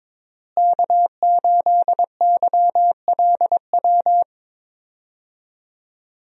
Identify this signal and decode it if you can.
Morse code, heard as a keyed tone, about 22 wpm, 700 Hz. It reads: K8YLW
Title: USA Call Signs - 5 Characters - Repeated 3x - 22wpm